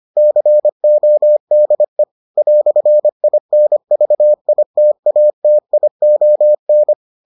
Morse code, heard as a keyed tone, about 25 wpm, 600 Hz. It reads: CODE "INVITATION